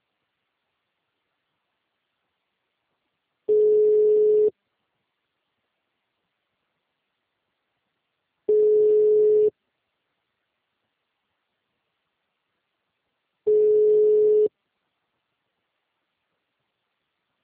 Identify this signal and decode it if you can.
Ringback tone (Europe, 425 Hz).